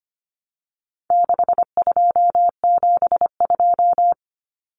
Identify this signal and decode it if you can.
Morse code, heard as a keyed tone, about 25 wpm, 700 Hz. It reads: 6272
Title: Common Cut Numbers - 4-Digit 25wpm